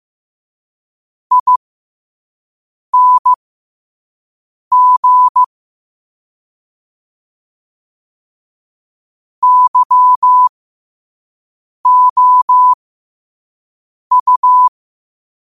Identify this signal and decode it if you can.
Morse code, heard as a keyed tone, about 15 wpm, spaced as for 5 wpm, 1000 Hz. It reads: ING YOU
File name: qotdfiles / 5wpm